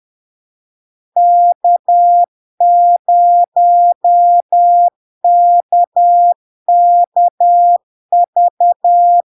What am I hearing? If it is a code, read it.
Morse code, heard as a keyed tone, about 10 wpm, 700 Hz. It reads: K0KKV
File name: LARK-FD-2022-Log-in-CW-10wpm-700-Hz